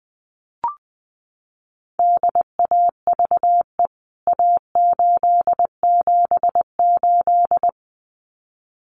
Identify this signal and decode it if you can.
Morse code, heard as a keyed tone, about 20 wpm, 700 Hz. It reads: DAVE A878
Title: CWT Exchange 20wpm